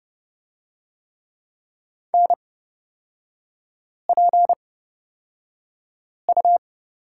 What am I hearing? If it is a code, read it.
Morse code, heard as a keyed tone, about 30 wpm, spaced as for 5 wpm, 700 Hz. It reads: NPU
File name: Lesson 37e - Letter x - Three Characters 30wpm